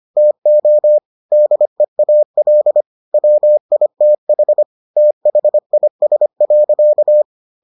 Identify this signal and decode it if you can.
Morse code, heard as a keyed tone, about 25 wpm, 600 Hz. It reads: TO DEAL WITH THIS.